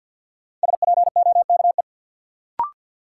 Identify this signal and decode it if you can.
Morse code, heard as a keyed tone, about 53 wpm, 700 Hz. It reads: SPOKE
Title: Sentences from Top 1200 Words - New Word Drill Alt - 50wpm